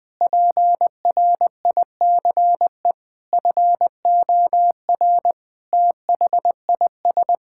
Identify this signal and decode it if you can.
Morse code, heard as a keyed tone, about 20 wpm, 700 Hz. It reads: PRICE FOR THIS
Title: Sentences from Top 1000 Words - Review Part 5 - 20wpm